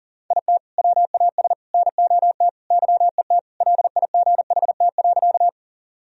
Morse code, heard as a keyed tone, about 40 wpm, 700 Hz. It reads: IT WAS NOT YET LIGHT.
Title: Sentences from Top 1000 Words - Review Part 1 - 40wpm